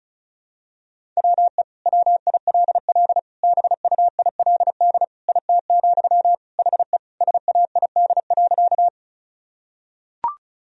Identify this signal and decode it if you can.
Morse code, heard as a keyed tone, about 36 wpm, 700 Hz. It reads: WE WILL BUILD IT, HE SAID.
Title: Sentences from Top 1000 Words - Review Part 1 - 35wpm